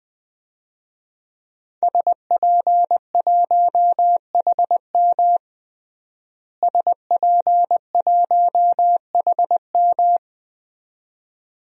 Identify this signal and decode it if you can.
Morse code, heard as a keyed tone, about 20 wpm, 700 Hz. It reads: SP1HM SP1HM
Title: DXCC-100 Call Signs - 5 Characters - Repeated 2x - 20wpm